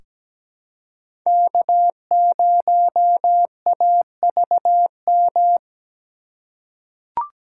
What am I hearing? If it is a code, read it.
Morse code, heard as a keyed tone, about 17 wpm, 700 Hz. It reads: K0AVM
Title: USA Call Signs - 5 Characters - 2x Character Spacing - 17wpm